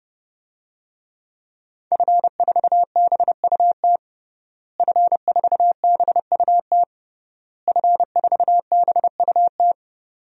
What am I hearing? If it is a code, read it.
Morse code, heard as a keyed tone, about 30 wpm, 700 Hz. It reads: F4BUT F4BUT F4BUT
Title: DXCC-100 Call Signs - 5 Characters - Repeated 3x - 30wpm